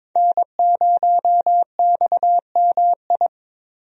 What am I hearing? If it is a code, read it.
Morse code, heard as a keyed tone, about 21 wpm, 700 Hz. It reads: N0XMI
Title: USA Call Signs - 5 Characters - Repeated 2x - 22wpm